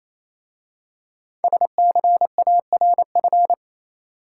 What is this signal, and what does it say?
Morse code, heard as a keyed tone, about 27 wpm, 700 Hz. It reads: SCARF
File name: Easy Words - Speed-Racing with Hints - 22wpm